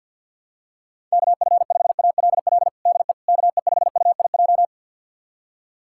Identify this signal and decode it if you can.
Morse code, heard as a keyed tone, about 50 wpm, 700 Hz. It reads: KF5ALL DE KE5UIJ
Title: USA Callsign DE Callsign 50wpm